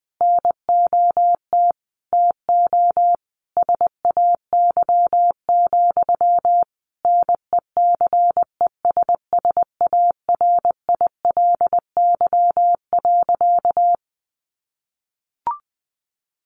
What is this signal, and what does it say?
Morse code, heard as a keyed tone, about 20 wpm, 700 Hz. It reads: NOT TO SAY, NECESSARILY.
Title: Sentences from Top 1400 Words 20wpm